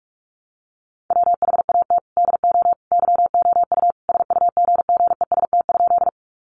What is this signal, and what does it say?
Morse code, heard as a keyed tone, about 44 wpm, 700 Hz. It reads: WHAT DO YOU SUGGEST?
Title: Sentences from Top 1300 Words 45wpm